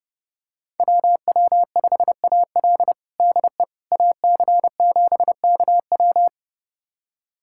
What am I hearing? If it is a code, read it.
Morse code, heard as a keyed tone, about 30 wpm, 700 Hz. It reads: WW5AL DE AC7KW
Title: USA Callsign DE Callsign - 5 Characters - 30wpm